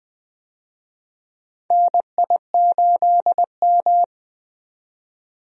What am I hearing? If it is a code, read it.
Morse code, heard as a keyed tone, about 21 wpm, 700 Hz. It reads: NI8M
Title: USA Call Signs - 4 Characters - 20wpm